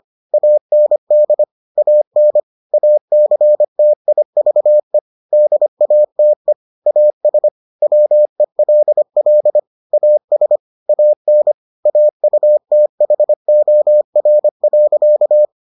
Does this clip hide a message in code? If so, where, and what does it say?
Yes: Morse code from 0.3 to 15.4 s, heard as a keyed tone, about 25 wpm, 600 Hz. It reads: AND AN ACTIVE DATE AS WELL AS AN AUTHOR.